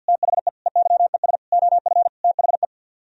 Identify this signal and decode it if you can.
Morse code, heard as a keyed tone, about 62 wpm, 700 Hz. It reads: THE EYES OF THE
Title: Speed-Racing - Sets of 5 Words 50wpm